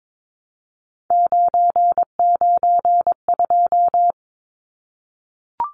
Morse code, heard as a keyed tone, about 22 wpm, 700 Hz. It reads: 992
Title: Numbers - 3-Digit 22wpm